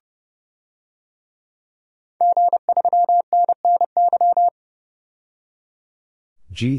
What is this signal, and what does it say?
Morse code, heard as a keyed tone, about 30 wpm, 700 Hz. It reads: G3NNY
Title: DXCC-100 Call Signs - 5 Characters - 30wpm